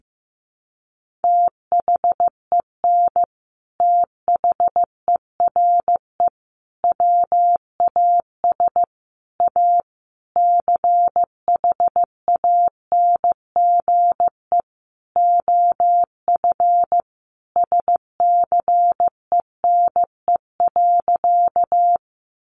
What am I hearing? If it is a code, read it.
Morse code, heard as a keyed tone, about 14 wpm, 700 Hz. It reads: THEN THERE WAS A CHANGE OF SCENE.
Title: Sentences from Top 1100 Words 15wpm